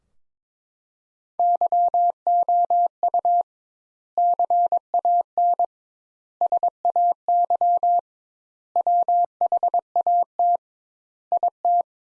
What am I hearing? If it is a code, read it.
Morse code, heard as a keyed tone, about 22 wpm, 700 Hz. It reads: YOU CAN SAY WHAT IT
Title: Sentences Top 200 Words - 22wpm